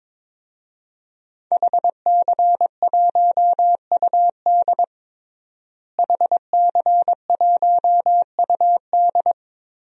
Morse code, heard as a keyed tone, about 22 wpm, 700 Hz. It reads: HC1UD HC1UD
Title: DXCC-100 Call Signs - 5 Characters - Repeated 3x - 22wpm